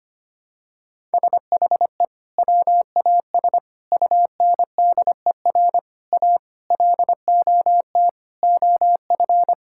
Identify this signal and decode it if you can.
Morse code, heard as a keyed tone, about 25 wpm, 700 Hz. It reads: SHE WAS UNDER A LOT OF
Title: Sentences from Top 1000 Words - Review Part 6 - 25wpm